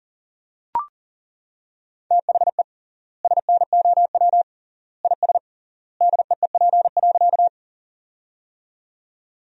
Morse code, heard as a keyed tone, about 40 wpm, 700 Hz. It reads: THE SNOW IS DEEP.
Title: 3x Word-Spacing - Sentences from Top 500 Words - 40wpm